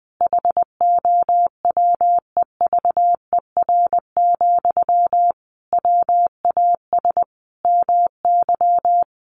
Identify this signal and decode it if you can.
Morse code, heard as a keyed tone, about 20 wpm, 700 Hz. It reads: HOWEVER, WAS MY
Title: Sentences from Top 1500 Words 20wpm